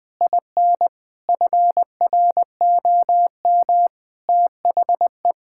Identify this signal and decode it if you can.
Morse code, heard as a keyed tone, about 20 wpm, 700 Hz. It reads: IN FROM THE
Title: Sentences from Top 1000 Words - Review Part 5 - 20wpm